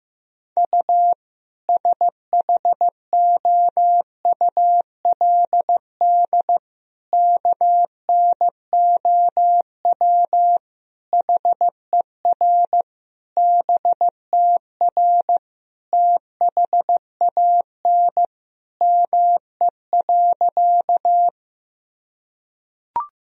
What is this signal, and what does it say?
Morse code, heard as a keyed tone, about 15 wpm, 700 Hz. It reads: U SHOULD KNOW HER BTR THAN ME.
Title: Sentences from Top 400 Words and Abbreviations 15wpm